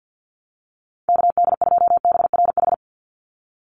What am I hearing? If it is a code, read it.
Morse code, heard as a keyed tone, about 50 wpm, 700 Hz. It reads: KD2BRH
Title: USA Call Signs 50wpm